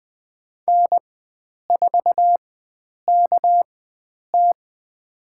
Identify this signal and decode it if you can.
Morse code, heard as a keyed tone, about 20 wpm, spaced as for 9 wpm, 700 Hz. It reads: N4KT
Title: USA Call Signs - 4 Characters - 4x Character Spacing - 20wpm